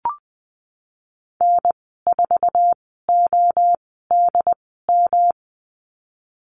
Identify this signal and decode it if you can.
Morse code, heard as a keyed tone, about 20 wpm, spaced as for 14 wpm, 700 Hz. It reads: N4ODM
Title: 128 Lesson 21g - Number 6 - Call Signs 20wpm